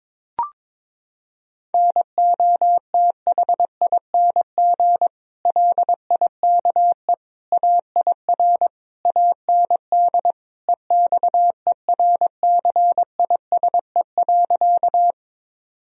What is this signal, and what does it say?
Morse code, heard as a keyed tone, about 22 wpm, 700 Hz. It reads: NOTHING LIKE AIR AND EXERCISE.
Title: Sentences from Top 1000 Words - Review Part 4 - 22wpm